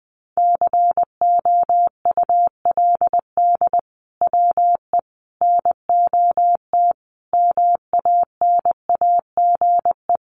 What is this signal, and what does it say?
Morse code, heard as a keyed tone, about 20 wpm, 700 Hz. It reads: COULD WE NOT MANAGE
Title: Sentences from Top 1400 Words 20wpm